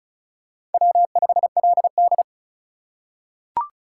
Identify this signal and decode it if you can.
Morse code, heard as a keyed tone, about 34 wpm, 700 Hz. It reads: W5LD
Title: USA Call Signs - 4 Characters - 4x Character Spacing - 35wpm